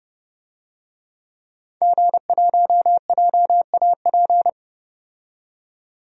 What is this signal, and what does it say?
Morse code, heard as a keyed tone, about 30 wpm, 700 Hz. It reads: G1JAP
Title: DXCC-100 Call Signs - 5 Characters - 30wpm